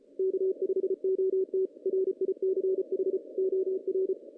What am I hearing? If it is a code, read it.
Morse code, heard as a keyed tone, about 34 wpm, 380 Hz. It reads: K5OT RICH OR
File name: 2016 JAN CW NAQP